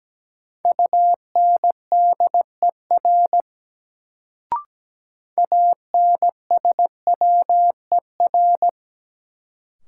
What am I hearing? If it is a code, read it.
Morse code, heard as a keyed tone, about 17 wpm, 700 Hz. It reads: UNDER ANSWER
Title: Sentences from Top 300 Words - New Word Drill - 17wpm